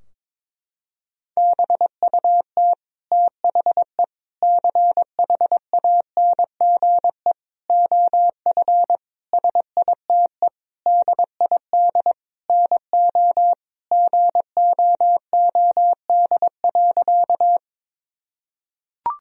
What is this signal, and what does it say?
Morse code, heard as a keyed tone, about 22 wpm, 700 Hz. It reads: BUT THE CHANGE OF SITE DID NO GOOD.
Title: Sentences from Top 1000 Words - Review Part 4 - 22wpm